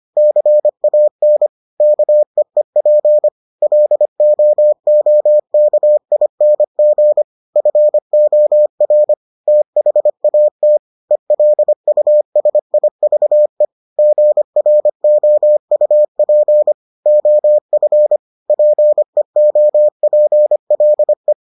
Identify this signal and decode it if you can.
Morse code, heard as a keyed tone, about 25 wpm, 600 Hz. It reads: CAN KEEP LOOKING FOR THAT ELUSIVE GROUP OF PEOPLE